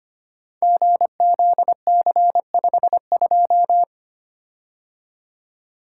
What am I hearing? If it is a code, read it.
Morse code, heard as a keyed tone, about 25 wpm, 700 Hz. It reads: GZC52
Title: Code Groups - 5 Letter:Numbers - 25wpm